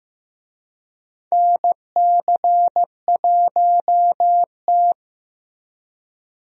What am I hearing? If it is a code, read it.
Morse code, heard as a keyed tone, about 15 wpm, 700 Hz. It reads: NC1T
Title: USA Call Signs - 4 Characters - Repeated 3x - 15wpm